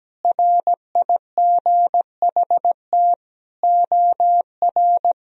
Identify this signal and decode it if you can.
Morse code, heard as a keyed tone, about 17 wpm, 700 Hz. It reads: RIGHT OR
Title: Sentences from Top 1000 Words - Review Part 2 - 17wpm